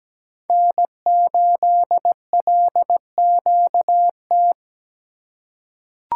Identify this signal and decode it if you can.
Morse code, heard as a keyed tone, about 17 wpm, 700 Hz. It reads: N8LQT
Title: USA Call Signs - 5 Characters - 4x Character Spacing - 17wpm